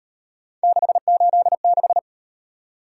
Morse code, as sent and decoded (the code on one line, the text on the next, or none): -.... ---.. -....
686